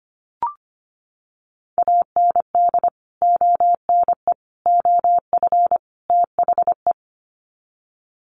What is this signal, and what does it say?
Morse code, heard as a keyed tone, about 25 wpm, 700 Hz. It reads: AND ONE OF THE
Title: Sets of 4 Words 25wpm